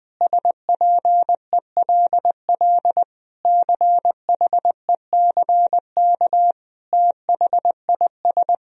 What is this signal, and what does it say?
Morse code, heard as a keyed tone, about 20 wpm, 700 Hz. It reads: SPELL CHECK THIS
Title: Sentences from Top 1000 Words - Review Part 3 - 20wpm